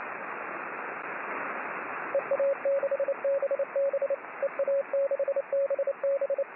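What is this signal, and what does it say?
Morse code, heard as a keyed tone, about 28 wpm, 550 Hz, with radio noise. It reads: EA6BB EA6BB
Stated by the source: Morse signal on the 6 m band.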